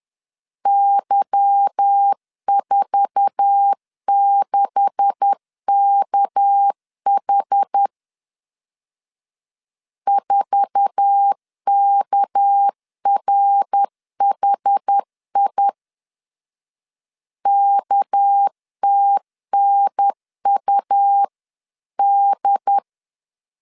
Morse code, heard as a keyed tone, about 10 wpm, 800 Hz. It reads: Y46KH 4KRHI KTNUD
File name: CW / DET / 1 9 1500C S A 10wpm 4511